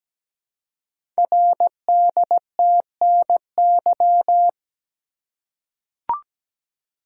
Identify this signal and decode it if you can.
Morse code, heard as a keyed tone, about 17 wpm, 700 Hz. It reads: RDTNY
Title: Code Groups - 5 Letters - 17wpm